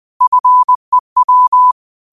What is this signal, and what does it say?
Morse code, heard as a keyed tone, about 20 wpm, 1000 Hz. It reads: FEW
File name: qotdfiles / 20wpm